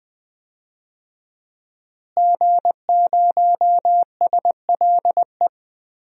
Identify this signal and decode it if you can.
Morse code, heard as a keyed tone, about 20 wpm, 700 Hz. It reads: G0SLE
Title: DXCC-100 Call Signs - 5 Characters - Repeated 3x - 20wpm